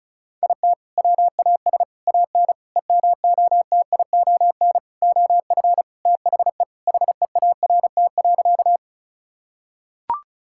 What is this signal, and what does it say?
Morse code, heard as a keyed tone, about 34 wpm, 700 Hz. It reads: IT WAS AN EMOTION OF THE HEART.
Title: Sentences from Top 1600 Words 35wpm